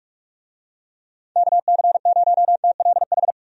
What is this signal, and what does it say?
Morse code, heard as a keyed tone, about 46 wpm, 700 Hz. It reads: KC0TLH
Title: USA Call Signs - 6 Characters - 45wpm